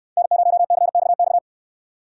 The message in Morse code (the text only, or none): N0CXX